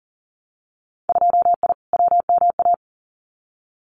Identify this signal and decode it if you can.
Morse code, heard as a keyed tone, about 40 wpm, 700 Hz. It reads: 2I WMA
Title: Winter Field Day Exchanges Encore 40wpm